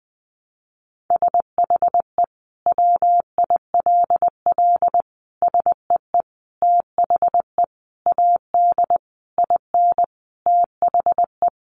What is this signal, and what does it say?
Morse code, heard as a keyed tone, about 20 wpm, 700 Hz. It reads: SHE WILL SEE THE AD IN THE
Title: Sentences from Top 1300 Words 20wpm